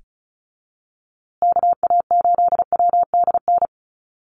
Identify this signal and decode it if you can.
Morse code, heard as a keyed tone, about 34 wpm, 700 Hz. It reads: KA8WDN
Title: USA Call Signs Repeated 3x 35wpm